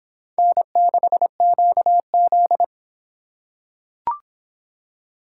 Morse code, heard as a keyed tone, about 24 wpm, 700 Hz. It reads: N6QZ